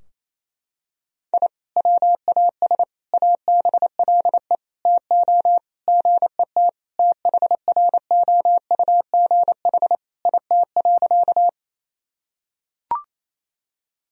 Morse code, heard as a keyed tone, about 28 wpm, 700 Hz. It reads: I WAS ABLE TO GET THROUGH IT.
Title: Sentences from Top 1000 Words - Review Part 2 - 28wpm